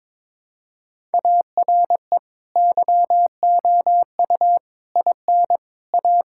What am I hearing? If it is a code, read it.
Morse code, heard as a keyed tone, about 22 wpm, 700 Hz. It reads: ARE YOU IN A
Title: Sentences from Top 1000 Words - Review Part 5 - 22wpm